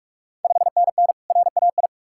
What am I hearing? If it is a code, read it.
Morse code, heard as a keyed tone, about 48 wpm, 690 Hz. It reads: 5NN RAI
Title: Ontario QSO Party Exchange 45wpm